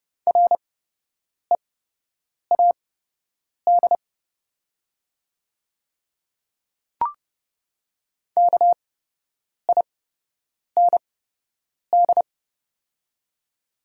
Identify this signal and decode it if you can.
Morse code, heard as a keyed tone, about 30 wpm, spaced as for 8 wpm, 700 Hz. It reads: READ KIND
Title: 8x Character Spacing - Top 200 Words - 30wpm